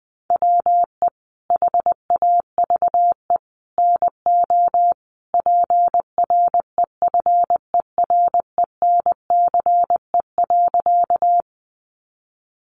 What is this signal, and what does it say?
Morse code, heard as a keyed tone, about 20 wpm, 700 Hz. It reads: WE HAVE NO PREFERENCE.
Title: Sentences from Top 1500 Words 20wpm